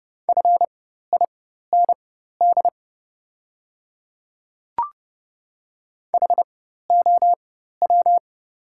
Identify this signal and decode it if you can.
Morse code, heard as a keyed tone, about 31 wpm, spaced as for 14 wpm, 700 Hz. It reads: FIND HOW
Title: 4x Character Spacing - Top 100 Words - 30wpm